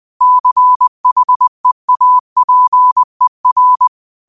Morse code, heard as a keyed tone, about 20 wpm, 1000 Hz. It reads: CHEAPER